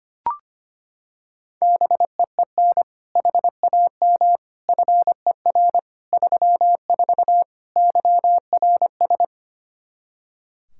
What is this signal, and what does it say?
Morse code, heard as a keyed tone, about 25 wpm, 700 Hz. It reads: BEEN HAM FER 34 YRS
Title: Practice QSO - Exchange 3 - More Info - 25wpm